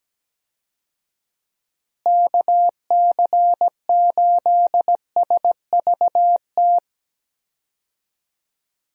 Morse code, heard as a keyed tone, about 17 wpm, 700 Hz. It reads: KC8SVT
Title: USA Call Signs - 6 Characters - Repeated 3x - 17wpm